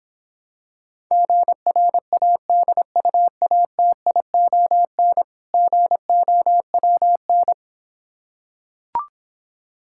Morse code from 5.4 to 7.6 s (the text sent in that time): GOWN